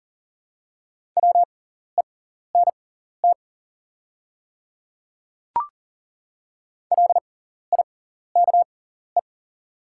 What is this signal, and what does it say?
Morse code, heard as a keyed tone, about 40 wpm, spaced as for 14 wpm, 700 Hz. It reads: WENT LIKE
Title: 6x Character Spacing - Top 200 Words - 40wpm